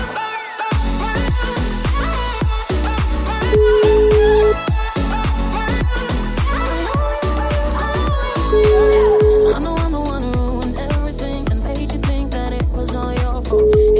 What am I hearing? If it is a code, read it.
Ringback tone (Europe, 425 Hz), with noise.